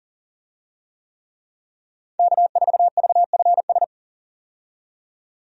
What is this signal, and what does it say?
Morse code, heard as a keyed tone, about 37 wpm, 700 Hz. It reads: K4VFS
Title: USA Call Signs - 5 Characters - 40wpm